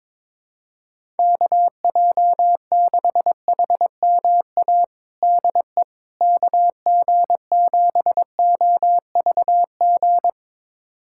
Morse code, heard as a keyed tone, about 22 wpm, 700 Hz. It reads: KJ6HMA DE KG7OVG
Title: USA Callsign DE Callsign - 6 Characters - 22wpm